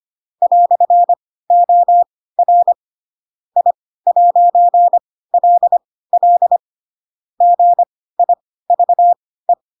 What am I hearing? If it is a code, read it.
Morse code, heard as a keyed tone, about 25 wpm, spaced as for 16 wpm, 700 Hz. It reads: "OR I'LL GIVE
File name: Pinocchio / Pinocchio ch7 10 25 ik1hge